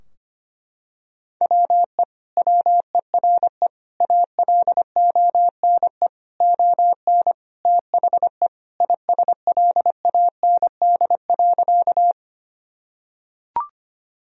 Morse code, heard as a keyed tone, about 25 wpm, 700 Hz. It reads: WE WERE ALONE ON THE ISLAND.
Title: Sentences from Top 1000 Words - Review Part 3 - 25wpm